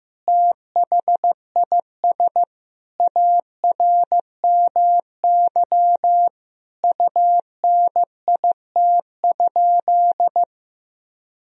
Morse code, heard as a keyed tone, about 15 wpm, 700 Hz. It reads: THIS ARMY UNIT?
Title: Sentences from Top 1000 Words - Review Part 2 - 15wpm